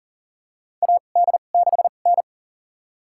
Morse code, heard as a keyed tone, about 38 wpm, spaced as for 28 wpm, 700 Hz. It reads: AD6N
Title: USA Call Signs - 4 Characters - 2x Character Spacing - 40wpm